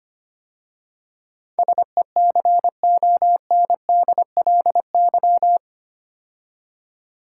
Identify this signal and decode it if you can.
Morse code, heard as a keyed tone, about 25 wpm, 700 Hz. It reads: SECONDLY
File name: 8 Letter Words - First 3 Letters Given - 25wpm